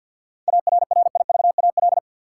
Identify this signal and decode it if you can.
Morse code, heard as a keyed tone, about 50 wpm, 700 Hz. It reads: ARRIVAL
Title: Seven Letter Words - 50wpm